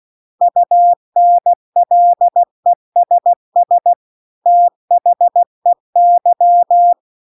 Morse code, heard as a keyed tone, about 16 wpm, 700 Hz. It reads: UNLESS THEY